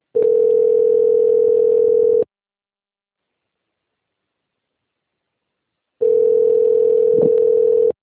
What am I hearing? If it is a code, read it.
Ringback tone (North America), 440 + 480 Hz.